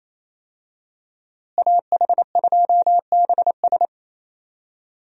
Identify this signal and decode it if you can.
Morse code, heard as a keyed tone, about 27 wpm, 700 Hz. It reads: AH2BS
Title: USA Call Signs - 5 Characters - Repeated 3x - 28wpm